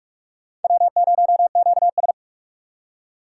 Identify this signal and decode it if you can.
Morse code, heard as a keyed tone, about 57 wpm, 700 Hz. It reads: W0QS